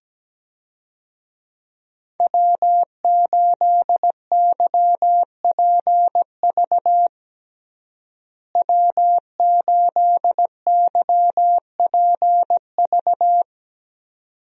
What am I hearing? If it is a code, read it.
Morse code, heard as a keyed tone, about 17 wpm, 700 Hz. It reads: W8YPV W8YPV
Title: USA Call Signs - 5 Characters - Repeated 2x - 17wpm